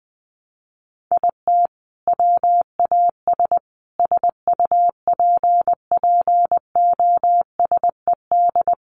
Morse code, heard as a keyed tone, about 20 wpm, 700 Hz. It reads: IT WAS SUPPOSED
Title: Sets of 5 Words Encore 20wpm